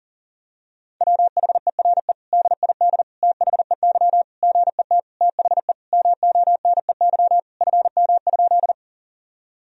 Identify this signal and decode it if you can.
Morse code, heard as a keyed tone, about 40 wpm, 700 Hz. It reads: WHERE DID THEY GET THE MONEY FM?
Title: Sentences from Top 400 Words and Abbreviations 40wpm